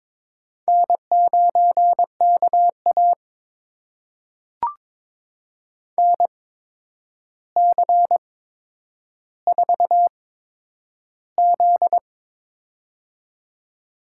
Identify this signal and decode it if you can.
Morse code, heard as a keyed tone, about 23 wpm, spaced as for 9 wpm, 700 Hz. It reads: N9KA NC4Z
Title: USA Call Signs - 4 Characters - 8x Character Spacing - 22wpm